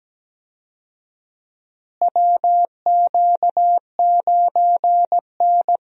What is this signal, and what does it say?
Morse code, heard as a keyed tone, about 17 wpm, 700 Hz. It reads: WQ9N
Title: USA Call Signs - 4 Characters - Repeated 2x - 17wpm